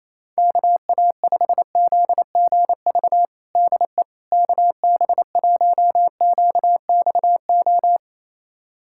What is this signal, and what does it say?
Morse code, heard as a keyed tone, about 28 wpm, 700 Hz. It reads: KA5ZGV DE KB1QXO
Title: USA Callsign DE Callsign - 6 Characters - 28wpm